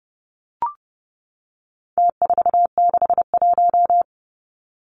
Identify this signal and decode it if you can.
Morse code, heard as a keyed tone, about 30 wpm, 700 Hz. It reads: T461
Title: Numbers - Contest Style - 4-Digit 30wpm